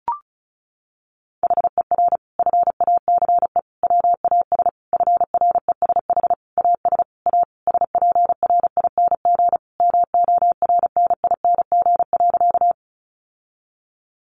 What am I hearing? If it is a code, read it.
Morse code, heard as a keyed tone, about 34 wpm, 700 Hz. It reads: HER FACE WAS FRESH AS A SPRING MORNING.
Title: Sentences from Top 1200 Words 35wpm